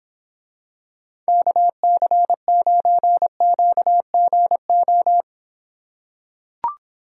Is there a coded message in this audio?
Yes: Morse code from 1.3 to 5.2 s, heard as a keyed tone, about 26 wpm, 700 Hz. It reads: KC9QGO